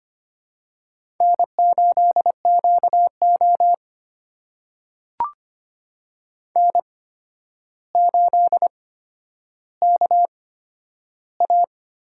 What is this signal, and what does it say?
Morse code, heard as a keyed tone, about 25 wpm, spaced as for 11 wpm, 700 Hz. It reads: N8QO N8KA
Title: USA Call Signs - 4 Characters - 8x Character Spacing - 25wpm